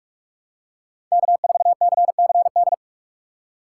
Morse code, heard as a keyed tone, about 46 wpm, 700 Hz. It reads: K4CCD
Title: USA Call Signs - 5 Characters - 3x Character Spacing - 45wpm